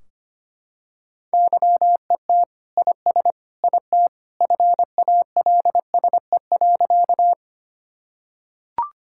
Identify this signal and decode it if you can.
Morse code, heard as a keyed tone, about 25 wpm, 700 Hz. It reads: YET IS IT FALSE.
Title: Sentences from Top 1600 Words 25wpm